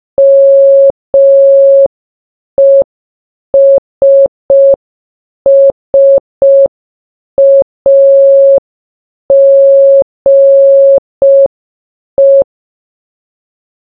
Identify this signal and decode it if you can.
Morse code, heard as a keyed tone, about 5 wpm, 550 Hz. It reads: MESSAGE
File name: baba-is-you-the-folder-megapack-linux-macos-windows-gamerip-2024 / dobyxyjbym / 30. Morse